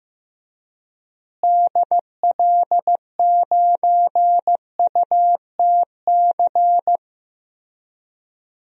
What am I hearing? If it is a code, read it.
Morse code, heard as a keyed tone, about 15 wpm, 700 Hz. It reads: DL9UTC
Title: DXCC-100 Call Signs - 6 Characters - Repeated 3x - 15wpm